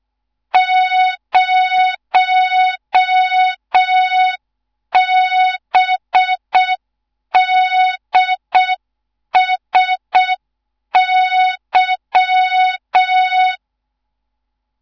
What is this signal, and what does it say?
Morse code, heard as a keyed tone, about 6 wpm, 3700 Hz. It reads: 0BDSY
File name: Morze-HA2MN / hang / 14 280 HA2MN